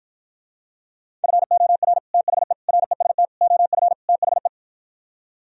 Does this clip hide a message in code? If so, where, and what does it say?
Yes: Morse code from 1.2 to 4.5 s, heard as a keyed tone, about 52 wpm, 700 Hz. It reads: FOR THE REST OF THE